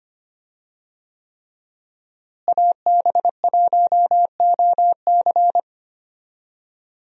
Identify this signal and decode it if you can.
Morse code, heard as a keyed tone, about 25 wpm, 700 Hz. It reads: AB1OC
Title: USA Call Signs - 5 Characters - Repeated 3x - 25wpm